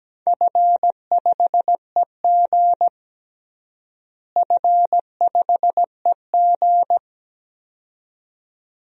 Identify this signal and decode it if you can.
Morse code, heard as a keyed tone, about 17 wpm, 700 Hz. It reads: F5EG F5EG
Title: DXCC-100 Call Signs - 4 Characters - Repeated 2x - 17wpm